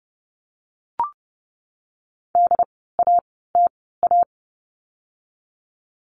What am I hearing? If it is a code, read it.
Morse code, heard as a keyed tone, about 30 wpm, spaced as for 17 wpm, 700 Hz. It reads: DATA
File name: Lesson 16f - Letter c - Words 30wpm